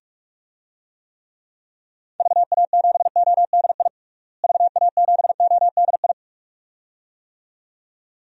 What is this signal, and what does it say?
Morse code, heard as a keyed tone, about 48 wpm, 700 Hz. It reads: VA7ODI VA7ODI
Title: DXCC-100 Call Signs - 6 Characters - Repeated 2x - 45wpm